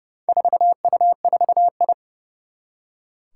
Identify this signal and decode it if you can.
Morse code, heard as a keyed tone, about 30 wpm, 700 Hz. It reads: 4U4I
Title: Four Challenging Letter-Number Combinations 30wpm